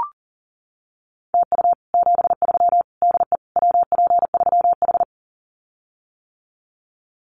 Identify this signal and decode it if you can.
Morse code, heard as a keyed tone, about 40 wpm, 700 Hz. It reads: TU 73 DE WP3H
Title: Common QSO Phrases Encore 40wpm